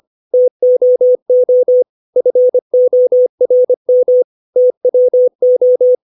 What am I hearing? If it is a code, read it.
Morse code, heard as a keyed tone, about 25 wpm, 500 Hz. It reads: TOO FORM TWO
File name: 632 words / 632 Words 25WPM